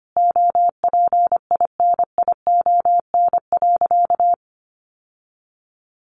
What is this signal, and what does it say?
Morse code, heard as a keyed tone, about 25 wpm, 700 Hz. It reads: OPINION.
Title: Sentences from Top 1200 Words 25wpm